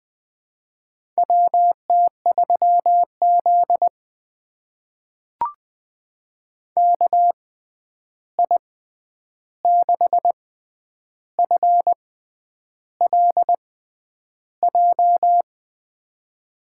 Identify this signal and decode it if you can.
Morse code, heard as a keyed tone, about 21 wpm, spaced as for 9 wpm, 700 Hz. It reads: WT3Z KI6FLJ
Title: USA Call Signs - 6x Character Spacing - 20wpm